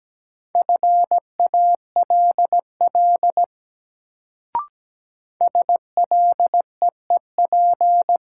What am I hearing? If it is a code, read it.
Morse code, heard as a keyed tone, about 17 wpm, 700 Hz. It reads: FALL SLEEP
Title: Sentences from Top 500 Words - New Word Drill - 17wpm